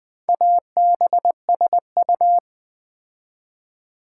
Morse code, heard as a keyed tone, about 20 wpm, 700 Hz. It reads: ABSU
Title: Build A Word - 2 Letters At A Time - 6-Letter Words - 20wpm